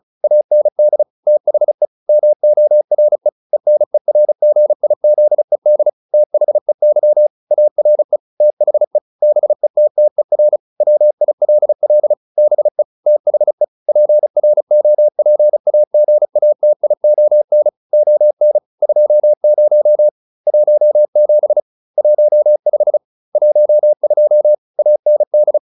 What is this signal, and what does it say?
Morse code, heard as a keyed tone, about 34 wpm, 600 Hz. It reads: AND THE MORE ENERGIZED THEY ARE THE BETTER WILL BE THE PROPAGATION ON 20 17 15 12 AND